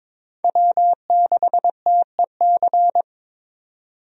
Morse code, heard as a keyed tone, about 22 wpm, 700 Hz. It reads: W6TEC
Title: USA Call Signs - 5 Characters - 22wpm